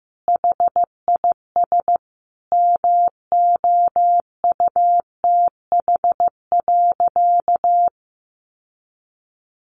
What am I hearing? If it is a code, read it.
Morse code, heard as a keyed tone, about 15 wpm, 700 Hz. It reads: HIS MOUTH.
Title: Sentences from Top 1500 Words 15wpm